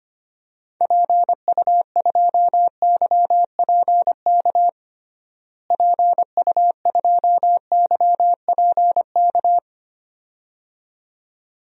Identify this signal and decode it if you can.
Morse code, heard as a keyed tone, about 25 wpm, 700 Hz. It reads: PU2YPK PU2YPK
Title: DXCC-100 Call Signs - 6 Characters - Repeated 2x - 25wpm